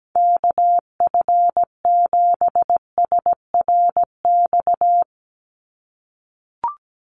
Morse code, heard as a keyed tone, about 17 wpm, 700 Hz. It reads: KF7SRX
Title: USA Call Signs 17wpm